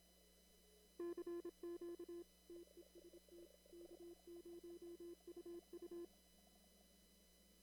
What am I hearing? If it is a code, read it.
Morse code, heard as a keyed tone, about 26 wpm, 340 Hz. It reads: CQ TEST K0UU
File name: CW / kt5j wpx cw-op-k5pi